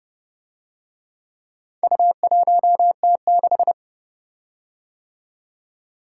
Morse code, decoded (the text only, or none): U1T6